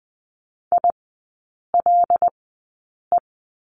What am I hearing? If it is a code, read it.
Morse code, heard as a keyed tone, about 20 wpm, spaced as for 8 wpm, 700 Hz. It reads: ILE
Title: 074 Lesson 13e - Letter L - Three Characters 20wpm